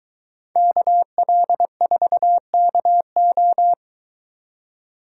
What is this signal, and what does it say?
Morse code, heard as a keyed tone, about 22 wpm, 700 Hz. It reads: KL4KO